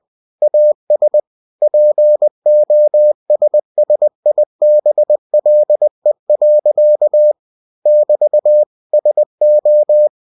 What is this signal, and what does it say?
Morse code, heard as a keyed tone, about 20 wpm, 600 Hz. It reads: AS POSSIBLE. <BT> SO
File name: cw-books / winnie-the-pooh / winnie-20wpm-page-0102